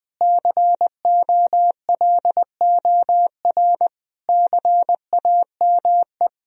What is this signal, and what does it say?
Morse code, heard as a keyed tone, about 20 wpm, 700 Hz. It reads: COLOR CAME
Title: Sentences from Top 1000 Words - Review Part 2 - 20wpm